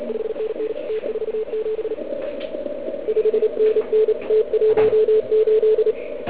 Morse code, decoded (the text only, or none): 5NN T47 5NN T18